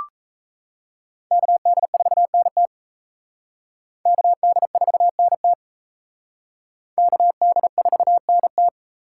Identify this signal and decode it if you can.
Morse code, heard as a keyed tone, about 37 wpm, 700 Hz. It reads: KD4NT KD4NT KD4NT